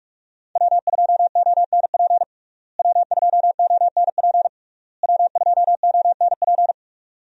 Morse code, heard as a keyed tone, about 43 wpm, 700 Hz. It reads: W2ONP W2ONP W2ONP
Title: USA Call Signs - 5 Characters - Repeated 3x - 45wpm